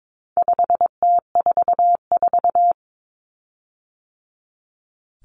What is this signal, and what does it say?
Morse code, heard as a keyed tone, about 22 wpm, 700 Hz. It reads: 5T44
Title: Common Cut Numbers - 4-Digit 22wpm